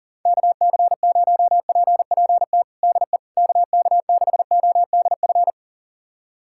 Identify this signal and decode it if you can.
Morse code, heard as a keyed tone, about 37 wpm, 700 Hz. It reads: KC0PPT DE KK6ODF
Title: USA Callsign DE Callsign 40wpm